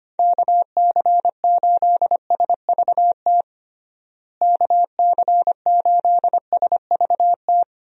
Morse code, heard as a keyed tone, about 25 wpm, 700 Hz. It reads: KC8SVT KC8SVT
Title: USA Call Signs - 6 Characters - Repeated 3x - 25wpm